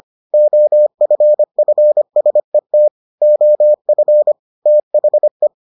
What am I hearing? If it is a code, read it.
Morse code, heard as a keyed tone, about 25 wpm, 600 Hz. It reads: OFFSET OF THE